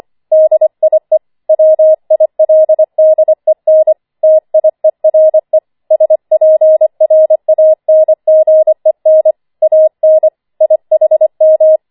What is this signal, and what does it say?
Morse code, heard as a keyed tone, about 25 wpm, 630 Hz. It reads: DIE WILDEN TIERE SPRANGEN AN IHM
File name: 25wpm